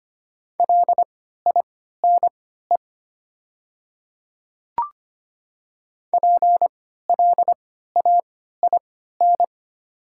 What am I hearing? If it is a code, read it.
Morse code, heard as a keyed tone, about 25 wpm, spaced as for 14 wpm, 690 Hz. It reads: LINE PLAIN
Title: 3x Character Spacing - Top 500 Words - 25wpm